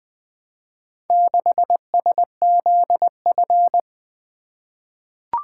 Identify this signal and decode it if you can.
Morse code, heard as a keyed tone, about 20 wpm, 700 Hz. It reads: 6SZF
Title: Four Letter-Number Combinations - 20wpm